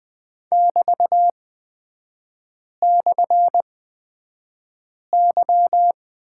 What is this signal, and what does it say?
Morse code, heard as a keyed tone, about 20 wpm, spaced as for 5 wpm, 700 Hz. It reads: <BT>/Y
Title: 224 Lesson 36e - BT - Three Characters 20wpm